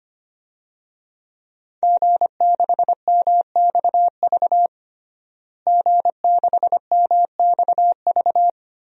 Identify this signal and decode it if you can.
Morse code, heard as a keyed tone, about 25 wpm, 700 Hz. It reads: G6MXV G6MXV
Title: DXCC-100 Call Signs - 5 Characters - Repeated 3x - 25wpm